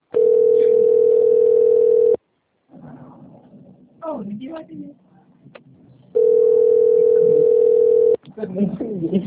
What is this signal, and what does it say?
Ringback tone (North America), 440 + 480 Hz, with speech.
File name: rg-1600-unknown- - - .84728